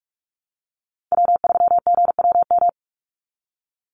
Morse code, heard as a keyed tone, about 44 wpm, 700 Hz. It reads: W3GWM
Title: Call Signs Repeated 2x - 45wpm